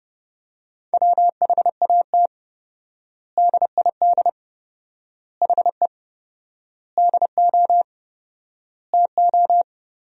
Morse code, heard as a keyed tone, about 30 wpm, 700 Hz. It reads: WHAT DID HE DO TO
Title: 4x Word-Spacing - Sentences from Top 100 Words - 30wpm